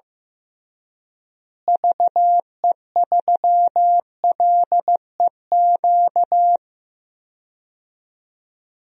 Morse code, heard as a keyed tone, about 15 wpm, 700 Hz. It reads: VE3LEQ
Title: DXCC-100 Call Signs - 6 Characters - Repeated 3x - 15wpm